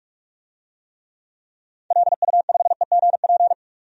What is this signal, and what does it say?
Morse code, heard as a keyed tone, about 46 wpm, 700 Hz. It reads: LU5EGP
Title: DXCC-100 Call Signs - 6 Characters - 45wpm